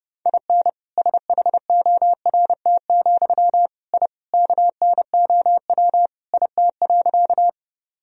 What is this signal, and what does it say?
Morse code, heard as a keyed tone, about 32 wpm, 700 Hz. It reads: IN SHORT, I KNOW IT.
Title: Sentences from Top 1000 Words - Review Part 2 - 30wpm